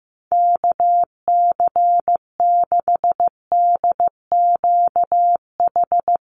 Morse code, heard as a keyed tone, about 15 wpm, 700 Hz. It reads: KC6DQH
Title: USA Call Signs 15wpm